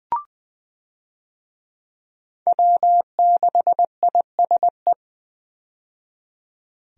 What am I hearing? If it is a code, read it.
Morse code, heard as a keyed tone, about 20 wpm, 700 Hz. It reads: W6ISE
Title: USA Call Signs - 5 Characters - 20wpm